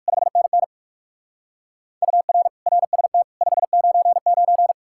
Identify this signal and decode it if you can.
Morse code, heard as a keyed tone, about 60 wpm, 700 Hz. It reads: 5NN UR RST 599